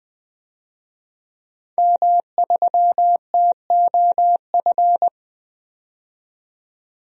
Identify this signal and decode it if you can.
Morse code, heard as a keyed tone, about 20 wpm, 700 Hz. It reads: M3TOF
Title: DXCC-100 Call Signs - 5 Characters - 20wpm